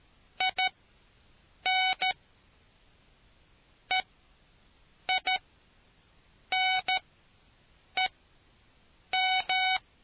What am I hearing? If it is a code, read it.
Morse code, heard as a keyed tone, about 13 wpm, spaced as for 6 wpm, 750 Hz. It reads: IN EINEM